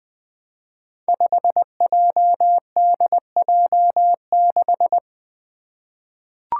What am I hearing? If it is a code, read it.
Morse code, heard as a keyed tone, about 20 wpm, 700 Hz. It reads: 5JDJ6
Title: Five Challenging Letter-Number Combinations 20wpm